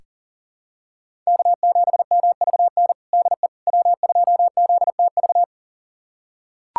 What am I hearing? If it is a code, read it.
Morse code, heard as a keyed tone, about 37 wpm, 700 Hz. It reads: K7MVN DE W2ZTV
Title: USA Callsign DE Callsign 40wpm